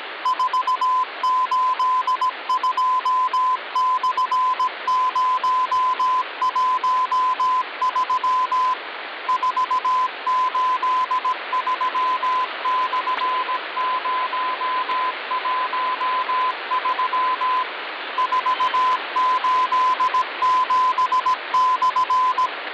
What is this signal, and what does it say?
Morse code, heard as a keyed tone, about 17 wpm, 1020 Hz, with radio noise. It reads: 482/013 483/013 487/
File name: original